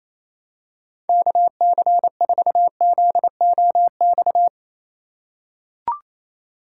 Morse code, heard as a keyed tone, about 28 wpm, 700 Hz. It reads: KC4ZOX
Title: USA Call Signs - 6 Characters - 2x Character Spacing - 28wpm